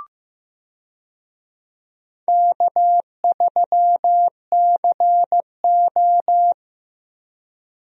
Morse code, heard as a keyed tone, about 15 wpm, 700 Hz. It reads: K3CO
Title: USA Call Signs - 4 Characters - Repeated 2x - 15wpm